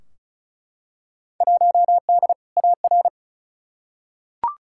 Morse code, heard as a keyed tone, about 36 wpm, 700 Hz. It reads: 1D AR
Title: Field Day Exchanges Encore - 35wpm